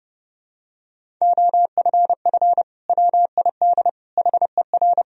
Morse code, heard as a keyed tone, about 32 wpm, 700 Hz. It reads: OFF WID HER
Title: Sentences from Top 300 Words and Abbreviations 30wpm